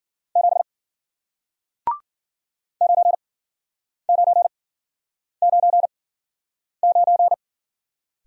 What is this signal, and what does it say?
Morse code, heard as a keyed tone, about 53 wpm, spaced as for 9 wpm, 700 Hz. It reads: 7 9999